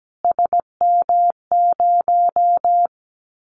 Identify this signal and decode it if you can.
Morse code, heard as a keyed tone, about 17 wpm, 700 Hz. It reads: SM0
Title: Callsign Prefixes for Top 25 DXCC - Repeated 3x - 17wpm